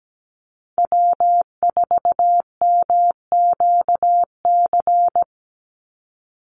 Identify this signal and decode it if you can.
Morse code, heard as a keyed tone, about 17 wpm, 700 Hz. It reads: W4MQC
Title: Call Signs - 17wpm